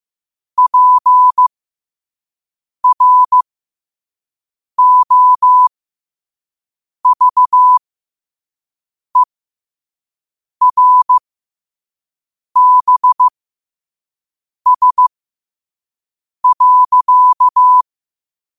Morse code, heard as a keyed tone, about 15 wpm, spaced as for 5 wpm, 1000 Hz. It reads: PROVERBS.
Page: New quotes every day in morse code at 5 Words per minute.